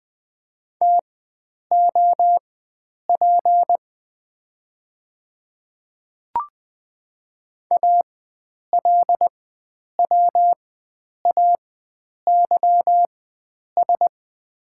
Morse code, heard as a keyed tone, about 20 wpm, spaced as for 9 wpm, 700 Hz. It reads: TOP ALWAYS
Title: 4x Character Spacing - Top 500 Words - 20wpm